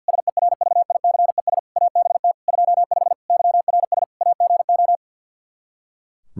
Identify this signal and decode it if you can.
Morse code, heard as a keyed tone, about 45 wpm, 700 Hz. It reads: SERVICES ABT 25 YRS AGO
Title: Practice QSO - Exchange 3 - More Info - 50wpm